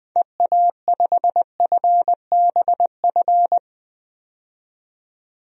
Morse code, heard as a keyed tone, about 20 wpm, 700 Hz. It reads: EA5FBF
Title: DXCC-100 Call Signs - 6 Characters - 20wpm